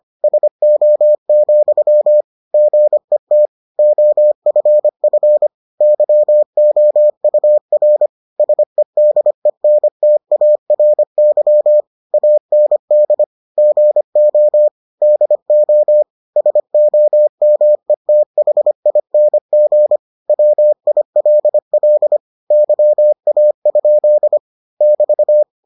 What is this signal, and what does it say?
Morse code, heard as a keyed tone, about 25 wpm, 600 Hz. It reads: SO, GET OFF YOUR SEDENTARY AND GO DO SOMETHING WILL YA? <BT>